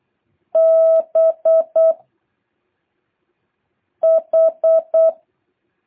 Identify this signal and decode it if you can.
Morse code, heard as a keyed tone, about 8 wpm, 650 Hz. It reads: B H
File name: jh8cbh / 2023cw